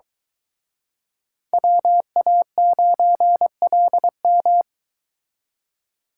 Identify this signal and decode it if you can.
Morse code, heard as a keyed tone, about 23 wpm, 700 Hz. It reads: WA9LM